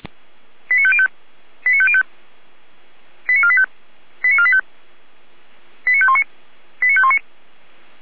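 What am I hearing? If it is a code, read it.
Five-tone selective call (ZVEI), heard as quick runs of five tones, tones of about 70 ms. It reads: 89505 89505 89485 89485 89519 89519
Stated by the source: ZVEI five-tone call received by SDR, from the audio server of feuerwehr-orlamuende.org (fire brigade Orlamünde).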